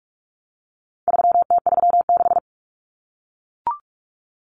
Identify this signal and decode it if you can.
Morse code, heard as a keyed tone, about 44 wpm, 700 Hz. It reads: 3T36
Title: Common Cut Numbers - 4-Digit 45wpm